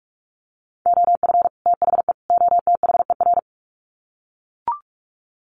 Morse code, heard as a keyed tone, about 44 wpm, 700 Hz. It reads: OF THE OTHER
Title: Sets of 3 Words 45wpm